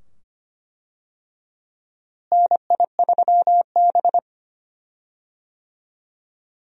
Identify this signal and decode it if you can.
Morse code, heard as a keyed tone, about 25 wpm, 700 Hz. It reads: NI3B
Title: USA Call Signs - 4 Characters - Repeated 2x - 25wpm